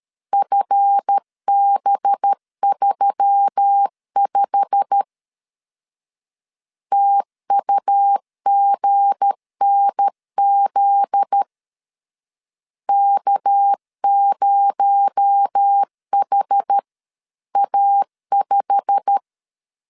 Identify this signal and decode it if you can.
Morse code, heard as a keyed tone, about 13 wpm, 800 Hz. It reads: FB35 TUGNZ K0HA5
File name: CW / DET / 1 9 1500C S B 12wpm 3819